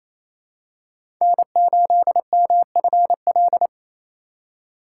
Morse code, heard as a keyed tone, about 28 wpm, 700 Hz. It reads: N8MFL
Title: USA Call Signs - 5 Characters - 6x Character Spacing - 28wpm